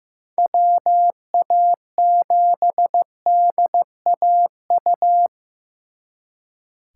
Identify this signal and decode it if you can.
Morse code, heard as a keyed tone, about 18 wpm, 700 Hz. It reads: WA7DAU
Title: Speed Racing - USA 2x3 Call Signs 15wpm